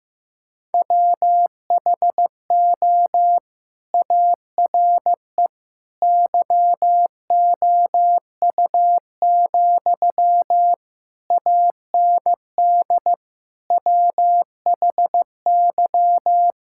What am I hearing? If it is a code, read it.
Morse code, heard as a keyed tone, about 16 wpm, 700 Hz. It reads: WHO ARE YOU, AND WHY
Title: Sentences from Top 1600 Words 15wpm